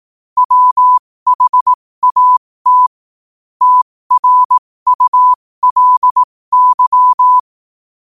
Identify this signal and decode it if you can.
Morse code, heard as a keyed tone, about 18 wpm, spaced as for 15 wpm, 1000 Hz. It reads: WHAT TRULY
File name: qotdfiles / 15wpm